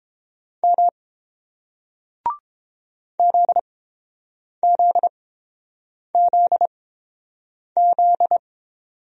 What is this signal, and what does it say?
Morse code, heard as a keyed tone, about 30 wpm, spaced as for 7 wpm, 690 Hz. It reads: M ZZZZ